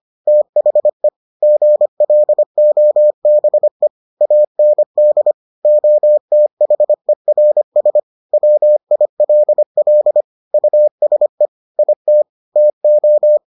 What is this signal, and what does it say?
Morse code, heard as a keyed tone, about 25 wpm, 600 Hz. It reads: THE GLOBE AND OTHERS WILL USE IT TO